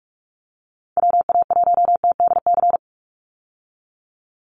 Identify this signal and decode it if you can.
Morse code, heard as a keyed tone, about 44 wpm, 700 Hz. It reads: WA1TDC
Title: USA Call Signs 45wpm